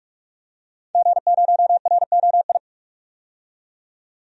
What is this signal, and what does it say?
Morse code, heard as a keyed tone, about 42 wpm, 700 Hz. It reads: G0ROI
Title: DXCC-100 Call Signs - 5 Characters - 45wpm